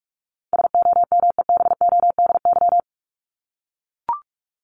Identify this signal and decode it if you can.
Morse code, heard as a keyed tone, about 44 wpm, 700 Hz. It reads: SOMEBODY
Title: Top 1751-2000 Words 45wpm